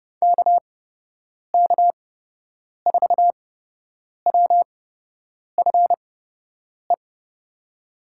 Morse code, heard as a keyed tone, about 31 wpm, spaced as for 8 wpm, 700 Hz. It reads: KK4WFE
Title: USA Call Signs - 8x Character Spacing - 30wpm